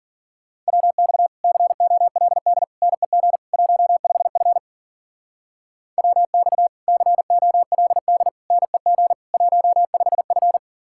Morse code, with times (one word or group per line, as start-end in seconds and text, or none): 0.7-1.3 WX
1.4-2.6 COLD
2.8-3.4 NEG
3.5-4.6 15F
6.0-6.7 WX
6.9-8.3 COLD
8.5-9.1 NEG
9.3-10.6 15F